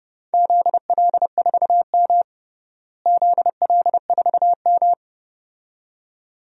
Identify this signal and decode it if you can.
Morse code, heard as a keyed tone, about 30 wpm, 700 Hz. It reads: ZL4M ZL4M
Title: DXCC-100 Call Signs - 4 Characters - Repeated 2x - 30wpm